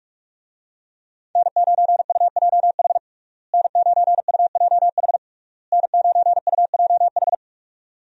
Morse code, heard as a keyed tone, about 41 wpm, 700 Hz. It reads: N9UJH N9UJH N9UJH
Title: USA Call Signs - 5 Characters - Repeated 3x - 45wpm